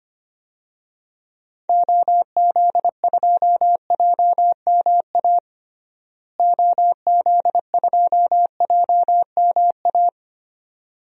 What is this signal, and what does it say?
Morse code, heard as a keyed tone, about 25 wpm, 700 Hz. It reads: OZ2JMA OZ2JMA
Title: DXCC-100 Call Signs - 6 Characters - Repeated 3x - 25wpm